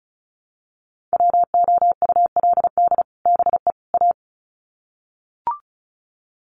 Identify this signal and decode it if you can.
Morse code, heard as a keyed tone, about 34 wpm, 700 Hz. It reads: WOULD BE A
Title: Sets of 3 Words 35wpm